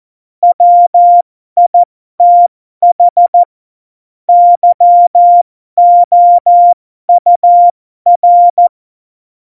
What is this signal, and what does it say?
Morse code, heard as a keyed tone, about 14 wpm, spaced as for 12 wpm, 700 Hz. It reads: WITH YOUR
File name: Morsekurs / Witze